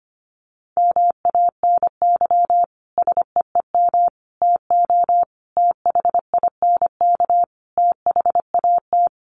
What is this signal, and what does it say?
Morse code, heard as a keyed tone, about 25 wpm, 700 Hz. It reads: MANY SEEM TO THINK THAT
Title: Sentences from Top 1400 Words 25wpm